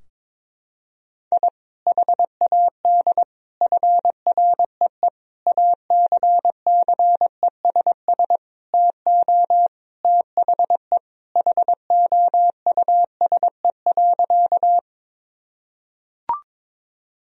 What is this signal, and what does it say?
Morse code, heard as a keyed tone, about 22 wpm, 700 Hz. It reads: I HAD FREE ACCESS TO THE HOUSE.
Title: Sentences from Top 1000 Words - Review Part 4 - 22wpm